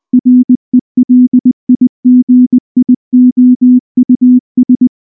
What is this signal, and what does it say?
Morse code, heard as a keyed tone, about 19 wpm, 250 Hz. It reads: RELIGIOUS
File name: morse / morse-training-20WPM-6